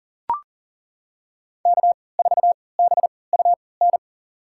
Morse code, heard as a keyed tone, about 38 wpm, spaced as for 22 wpm, 700 Hz. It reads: K4BUN
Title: USA Call Signs - 3x Character Spacing - 40wpm